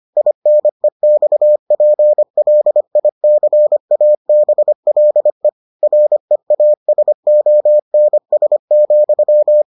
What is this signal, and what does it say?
Morse code, heard as a keyed tone, about 25 wpm, 600 Hz. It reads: INEXPLICABLE REASONS,